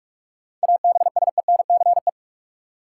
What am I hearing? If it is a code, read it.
Morse code, heard as a keyed tone, about 45 wpm, 700 Hz. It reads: ABSENCE
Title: Sets of 4 Words - New Word Drill - 45wpm